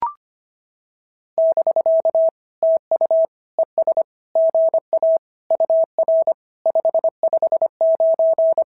Morse code, heard as a keyed tone, about 24 wpm, 650 Hz. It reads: <BK> TU ES GA UR 559
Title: SOTA QSO Practice 25wpm